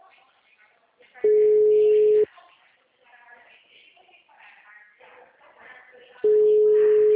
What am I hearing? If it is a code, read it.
Ringback tone (Europe, 425 Hz), with speech.